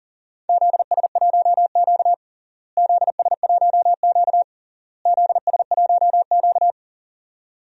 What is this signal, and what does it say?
Morse code, heard as a keyed tone, about 40 wpm, 700 Hz. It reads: ZS1Q ZS1Q ZS1Q
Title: DXCC-100 Call Signs - 4 Characters - Repeated 3x - 40wpm